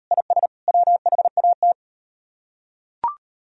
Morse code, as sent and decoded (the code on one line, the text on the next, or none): .. ... / .-- .... .- -
IS WHAT